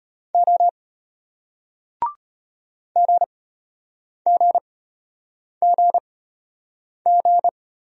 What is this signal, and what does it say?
Morse code, heard as a keyed tone, about 34 wpm, spaced as for 8 wpm, 700 Hz. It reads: O GGGG